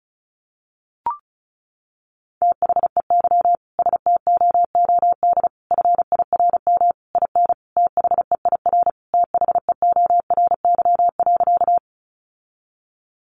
Morse code, heard as a keyed tone, about 34 wpm, 700 Hz. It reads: THEY STOOD FIRM IN THEIR THEORY.
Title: Sentences from Top 1400 Words 35wpm